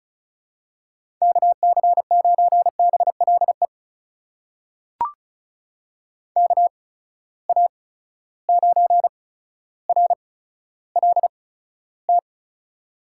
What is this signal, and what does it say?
Morse code, heard as a keyed tone, about 36 wpm, spaced as for 15 wpm, 700 Hz. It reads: KC9BLE KA9RLT
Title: USA Call Signs - 8x Character Spacing - 35wpm